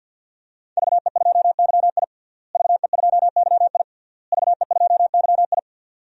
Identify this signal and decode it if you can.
Morse code, heard as a keyed tone, about 47 wpm, 700 Hz. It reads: VE2YI VE2YI VE2YI
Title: DXCC-100 Call Signs - 5 Characters - Repeated 3x - 50wpm